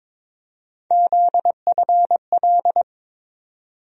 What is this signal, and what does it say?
Morse code, heard as a keyed tone, about 22 wpm, 700 Hz. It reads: ZFL
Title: USA Call Sign Postfixes - 3 Letters 22wpm